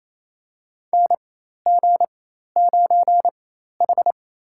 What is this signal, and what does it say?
Morse code, heard as a keyed tone, about 28 wpm, spaced as for 13 wpm, 700 Hz. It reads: NG9H
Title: USA Call Signs - 4 Characters - 4x Character Spacing - 28wpm